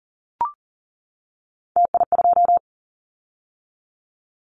Morse code, heard as a keyed tone, about 38 wpm, 690 Hz. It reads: TI2
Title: Callsign Prefixes for Top 100 DXCC - 40wpm